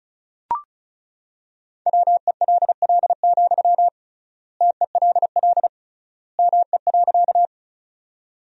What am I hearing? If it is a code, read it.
Morse code, heard as a keyed tone, about 34 wpm, 700 Hz. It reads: WELL, TELL ME.
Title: 3x Word-Spacing - Sentences from Top 200 Words - 35wpm